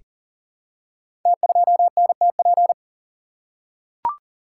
Morse code, heard as a keyed tone, about 38 wpm, 700 Hz. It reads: T2NTP
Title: Code Groups - 5 Letter:Numbers - 40wpm